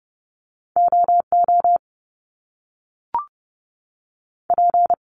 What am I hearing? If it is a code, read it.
Morse code, heard as a keyed tone, about 30 wpm, 700 Hz. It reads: OO P
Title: Top Two Letter Combinations in Words 30wpm